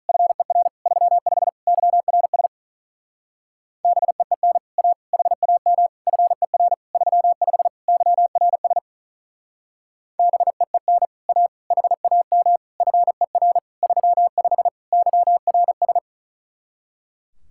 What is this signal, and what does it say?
Morse code, heard as a keyed tone, about 44 wpm, 700 Hz. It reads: FER 35 YRS BEEN A HAM FER 35 YRS BEEN A HAM FER 35 YRS